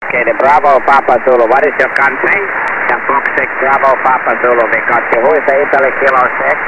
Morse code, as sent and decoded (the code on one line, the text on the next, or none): none